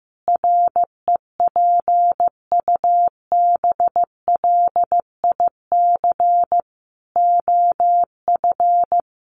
Morse code, heard as a keyed tone, about 15 wpm, 700 Hz. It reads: REPUBLIC OF
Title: Sentences from Top 1500 Words 15wpm